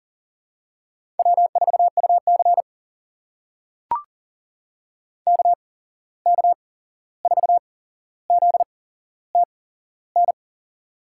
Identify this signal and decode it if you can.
Morse code, heard as a keyed tone, about 40 wpm, spaced as for 15 wpm, 700 Hz. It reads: W4UC KK4ZTN
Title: USA Call Signs - 8x Character Spacing - 40wpm